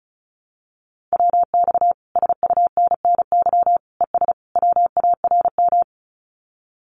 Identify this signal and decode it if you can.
Morse code, heard as a keyed tone, about 34 wpm, 700 Hz. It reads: WX SUNNY ES WARM
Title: Common QSO Phrases Encore 35wpm